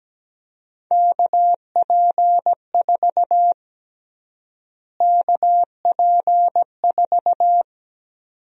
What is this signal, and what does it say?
Morse code, heard as a keyed tone, about 17 wpm, 700 Hz. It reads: KP4 KP4
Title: USA Call Sign Prefixes - Repeated 2x - 17wpm